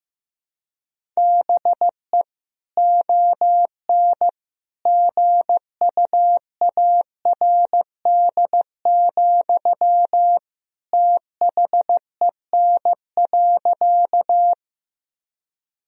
Morse code, heard as a keyed tone, about 15 wpm, 700 Hz. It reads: BE ON GUARD, THEN.
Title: Sentences from Top 1000 Words - Review Part 4 - 15wpm